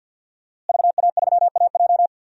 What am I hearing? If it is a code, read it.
Morse code, heard as a keyed tone, about 47 wpm, 700 Hz. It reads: VA3AJ
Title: DXCC-100 Call Signs - 5 Characters - Repeated 3x - 50wpm